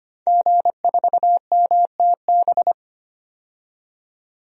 Morse code, heard as a keyed tone, about 25 wpm, 700 Hz. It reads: G4MTB
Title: DXCC-100 Call Signs - 5 Characters - 25wpm